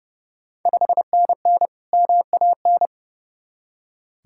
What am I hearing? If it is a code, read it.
Morse code, heard as a keyed tone, about 32 wpm, 700 Hz. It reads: 5NN MAN
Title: Ontario QSO Party Exchange 30wpm